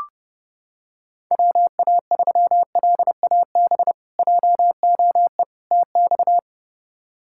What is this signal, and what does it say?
Morse code, heard as a keyed tone, about 30 wpm, 700 Hz. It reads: WA3LAB JOE TX
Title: K1USN SST QSO Practice - 30wpm